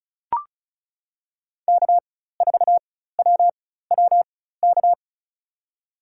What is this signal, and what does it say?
Morse code, heard as a keyed tone, about 34 wpm, spaced as for 16 wpm, 700 Hz. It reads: K4WWK
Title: USA Call Signs - 5 Characters - 4x Character Spacing - 35wpm